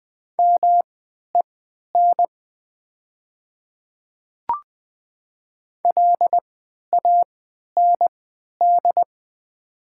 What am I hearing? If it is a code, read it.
Morse code, heard as a keyed tone, about 20 wpm, spaced as for 11 wpm, 700 Hz. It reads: MEN LAND
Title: 3x Character Spacing - Top 300 Words - 20wpm